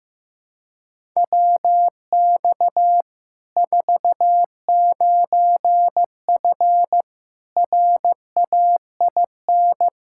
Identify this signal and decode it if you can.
Morse code, heard as a keyed tone, about 15 wpm, 700 Hz. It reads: WX 49F RAIN
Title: Practice QSO - Exchange 2 - Info - 15wpm